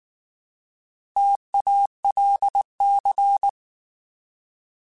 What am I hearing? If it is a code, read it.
Morse code, heard as a keyed tone, about 19 wpm, 800 Hz. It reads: TALC